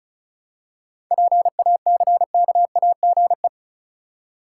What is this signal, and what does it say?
Morse code, heard as a keyed tone, about 36 wpm, 700 Hz. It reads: PACKAGE
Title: Seven Letter Words - 35wpm